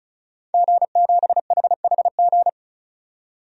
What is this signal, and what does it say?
Morse code, heard as a keyed tone, about 36 wpm, 700 Hz. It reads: G7HHG
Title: DXCC-100 Call Signs - 5 Characters - 35wpm